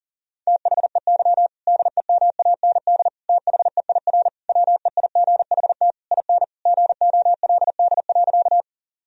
Morse code, heard as a keyed tone, about 37 wpm, 700 Hz. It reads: THEY DEMAND THEIR WEIGHT IN GOLD.
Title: Sentences from Top 1000 Words - Review Part 3 - 40wpm